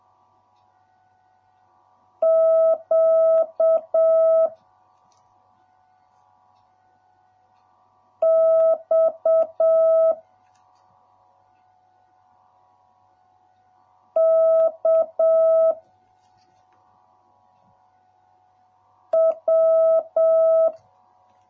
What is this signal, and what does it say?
Morse code, heard as a keyed tone, about 7 wpm, 650 Hz. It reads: Q X K W